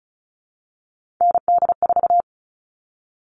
Morse code, heard as a keyed tone, about 34 wpm, 700 Hz. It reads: ND4
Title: All Cut Numbers - 3-Digit 35wpm